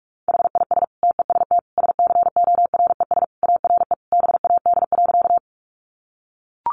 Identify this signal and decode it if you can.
Morse code, heard as a keyed tone, about 44 wpm, 700 Hz. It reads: HIS TEST SCORES ARE BAD.
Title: Additional Easy Phrases 45wpm